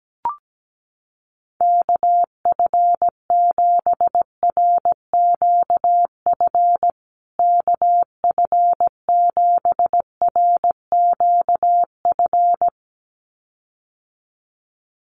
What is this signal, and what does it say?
Morse code, heard as a keyed tone, about 17 wpm, 700 Hz. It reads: KF7RQF KF7RQF
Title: USA Call Signs Repeated 2x 17wpm